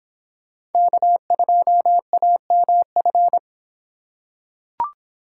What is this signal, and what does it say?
Morse code, heard as a keyed tone, about 25 wpm, 700 Hz. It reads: K2AMF